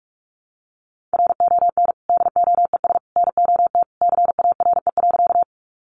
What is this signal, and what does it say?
Morse code, heard as a keyed tone, about 44 wpm, 700 Hz. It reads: RON DOES NOT CARE.
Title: Additional Easy Phrases 45wpm